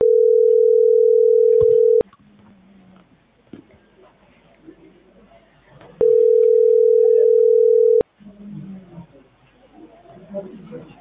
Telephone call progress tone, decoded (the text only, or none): ringback tone